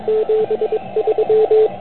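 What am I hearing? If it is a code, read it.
Morse code, heard as a keyed tone, about 22 wpm, 440 Hz. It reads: 73
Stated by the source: Morse code from the OK QRP zavod 2018 contest, Chrudim.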